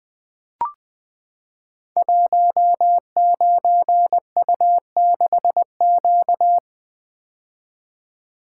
Morse code, heard as a keyed tone, about 20 wpm, 700 Hz. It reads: 19U6Q
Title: Code Groups - 5 Letter:Numbers - 20wpm